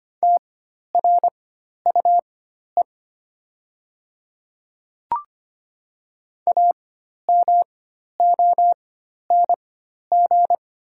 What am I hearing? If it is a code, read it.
Morse code, heard as a keyed tone, about 25 wpm, spaced as for 12 wpm, 700 Hz. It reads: TRUE AMONG
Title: 4x Character Spacing - Top 500 Words - 25wpm